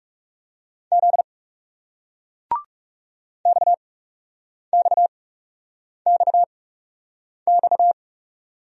Morse code, heard as a keyed tone, about 38 wpm, spaced as for 8 wpm, 700 Hz. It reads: Z XXXX